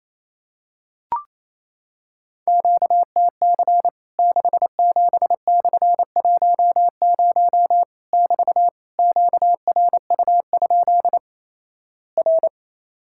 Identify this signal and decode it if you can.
Morse code, heard as a keyed tone, about 28 wpm, 700 Hz. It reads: QTC 67/10 <BT> QRU? R
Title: Worked All Europe DX Contest - QTC Exchange - 28wpm